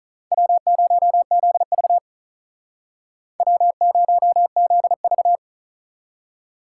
Morse code, heard as a keyed tone, about 44 wpm, 700 Hz. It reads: W0ZV W0ZV